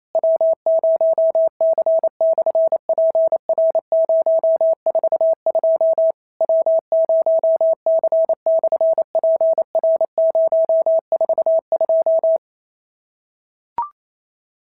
Morse code, heard as a keyed tone, about 25 wpm, 650 Hz. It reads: W0C/PR042 W0C/PR042
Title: SOTA QSO Practice 28wpm